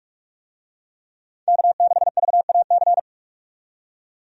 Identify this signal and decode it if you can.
Morse code, heard as a keyed tone, about 43 wpm, 700 Hz. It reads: K6VAC
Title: USA Call Signs - 5 Characters - 45wpm